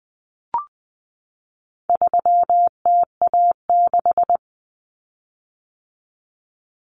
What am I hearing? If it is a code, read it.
Morse code, heard as a keyed tone, about 20 wpm, 700 Hz. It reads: 3TA6
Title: Common Cut Numbers - 4-Digit 20wpm